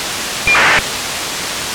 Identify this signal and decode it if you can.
ACARS aircraft data.